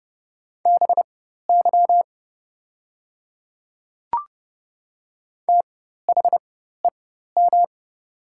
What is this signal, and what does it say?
Morse code, heard as a keyed tone, about 30 wpm, spaced as for 14 wpm, 700 Hz. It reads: BY THEM
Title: 4x Character Spacing - Top 200 Words - 30wpm